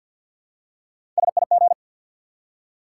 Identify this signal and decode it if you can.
Morse code, heard as a keyed tone, about 53 wpm, 700 Hz. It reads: SIG
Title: Top 180 QSO Elements 50wpm